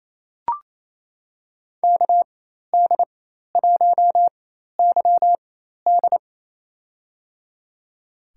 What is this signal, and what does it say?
Morse code, heard as a keyed tone, about 28 wpm, spaced as for 13 wpm, 700 Hz. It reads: KD1YD
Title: USA Call Signs - 5 Characters - 4x Character Spacing - 28wpm